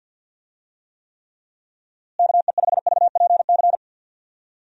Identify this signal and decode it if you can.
Morse code, heard as a keyed tone, about 48 wpm, 700 Hz. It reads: KE5VPC
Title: USA Call Signs - 6 Characters - Repeated 3x - 50wpm